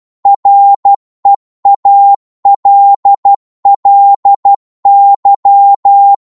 Morse code, heard as a keyed tone, about 12 wpm, 800 Hz. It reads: REALLY
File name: Morse / psg-12wpm